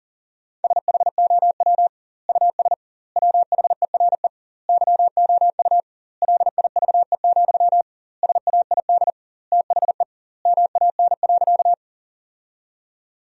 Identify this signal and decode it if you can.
Morse code, heard as a keyed tone, about 37 wpm, 700 Hz. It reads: SHOW US WHERE YOU LIVE, SAID THE MAN.
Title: 2x Word-Spacing - Sentences from Top 200 Words - 40wpm